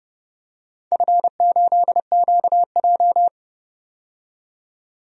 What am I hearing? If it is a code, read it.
Morse code, heard as a keyed tone, about 30 wpm, 700 Hz. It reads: F8QJ
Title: DXCC-100 Call Signs - 4 Characters - 30wpm